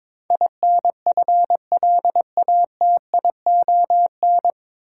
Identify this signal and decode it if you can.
Morse code, heard as a keyed tone, about 22 wpm, 700 Hz. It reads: INFLATION
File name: 9 Letter Words - First 3 Letters Given - 22wpm